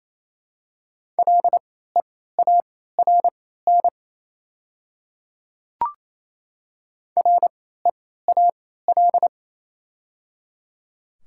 Morse code, heard as a keyed tone, about 28 wpm, spaced as for 16 wpm, 700 Hz. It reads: LEARN REAL
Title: 3x Character Spacing - Top 300 Words - 28wpm